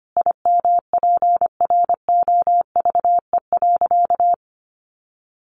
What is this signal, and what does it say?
Morse code, heard as a keyed tone, about 25 wpm, 700 Hz. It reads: IMPROVE.
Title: Sentences from Top 800 Words 25wpm